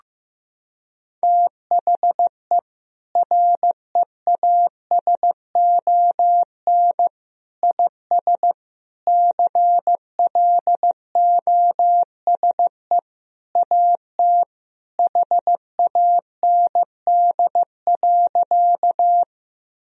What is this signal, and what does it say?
Morse code, heard as a keyed tone, about 16 wpm, 700 Hz. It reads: THE REASON IS CLOSE AT HAND.
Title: Sentences from Top 1000 Words - Review Part 1 - 15wpm